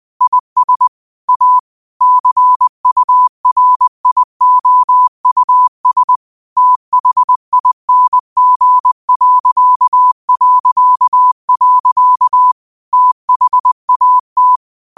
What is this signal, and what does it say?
Morse code, heard as a keyed tone, about 20 wpm, 1000 Hz. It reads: IS A CURIOUS THING... THAT
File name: qotdfiles / 20wpm